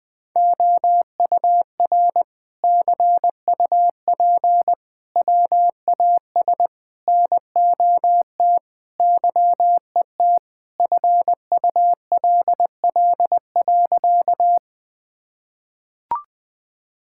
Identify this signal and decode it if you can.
Morse code, heard as a keyed tone, about 20 wpm, 700 Hz. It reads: OUR CUP WAS NOT YET FULL.
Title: Sentences from Top 1600 Words 20wpm